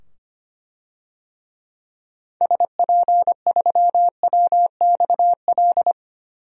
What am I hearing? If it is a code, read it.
Morse code, heard as a keyed tone, about 25 wpm, 690 Hz. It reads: SP3WXL
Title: DXCC-100 Call Signs - 6 Characters - 25wpm